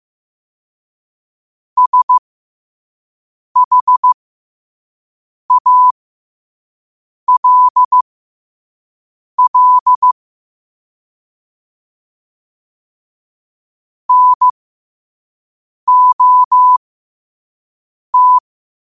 Morse code, heard as a keyed tone, about 15 wpm, spaced as for 5 wpm, 1000 Hz. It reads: SHALL NOT
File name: qotdfiles / 5wpm